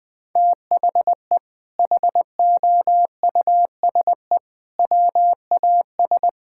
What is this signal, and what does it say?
Morse code, heard as a keyed tone, about 21 wpm, 700 Hz. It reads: THE HOUSE WAS
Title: Sentences from Top 1000 Words - Review Part 6 - 20wpm